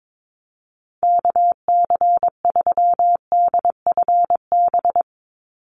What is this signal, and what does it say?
Morse code, heard as a keyed tone, about 22 wpm, 700 Hz. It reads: KC3DFB
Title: USA Call Signs 22wpm